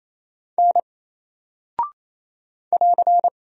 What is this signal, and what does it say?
Morse code, heard as a keyed tone, about 28 wpm, 700 Hz. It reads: N <AR>
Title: Single Character (Letter:Number:Punctuation:Prosign) Alt 28wpm